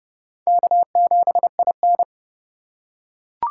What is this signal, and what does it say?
Morse code, heard as a keyed tone, about 27 wpm, 700 Hz. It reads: K7IN